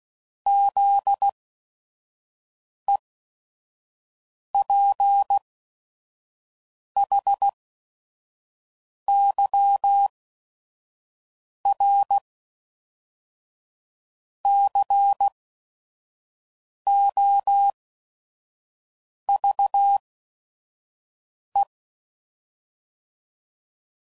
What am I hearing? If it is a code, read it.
Morse code, heard as a keyed tone, about 16 wpm, spaced as for 5 wpm, 800 Hz. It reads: ZEPHYRCOVE
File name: CW Code Groups @5WPM / fromSM's / Cities